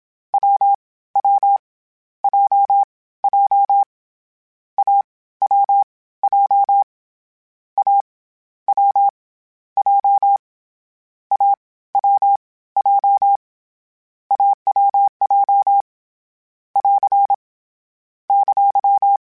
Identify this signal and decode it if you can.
Morse code, heard as a keyed tone, about 27 wpm, 800 Hz. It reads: W W J J A W J A W J A W J A W J <AR> !
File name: CW Code Groups-mp3s / Sets mp3s / AUV sp @25